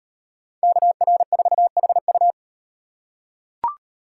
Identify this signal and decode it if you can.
Morse code, heard as a keyed tone, about 36 wpm, 700 Hz. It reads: KR4HU